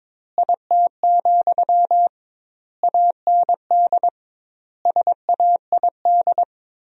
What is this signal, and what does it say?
Morse code, heard as a keyed tone, about 21 wpm, 700 Hz. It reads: IT, AND SAID
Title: 2x Word-Spacing - Sentences from Top 500 Words - 22wpm